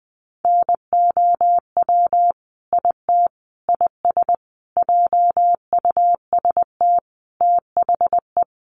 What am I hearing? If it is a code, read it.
Morse code, heard as a keyed tone, about 20 wpm, 700 Hz. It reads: NOW IT IS JUST THE
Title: Sentences from Top 1300 Words 20wpm